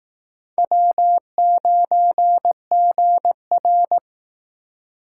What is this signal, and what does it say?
Morse code, heard as a keyed tone, about 18 wpm, 700 Hz. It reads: W9GR
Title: Speed Racing - USA 1x2 Call Signs 15wpm